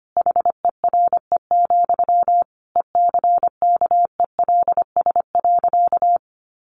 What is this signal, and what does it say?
Morse code, heard as a keyed tone, about 25 wpm, 700 Hz. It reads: HERE, ECKELS.
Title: A Sound Of Thunder 25wpm